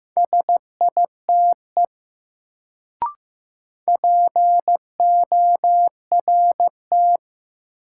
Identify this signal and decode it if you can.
Morse code, heard as a keyed tone, about 15 wpm, 700 Hz. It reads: SITE PORT
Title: Sentences from Top 1000 Words - Review Part 1 - New Word Drill - 15wpm